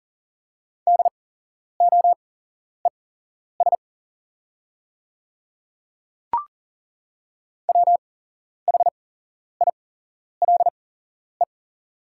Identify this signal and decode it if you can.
Morse code, heard as a keyed tone, about 40 wpm, spaced as for 11 wpm, 690 Hz. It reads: DOES WHILE
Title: 8x Character Spacing - Top 300 Words - 40wpm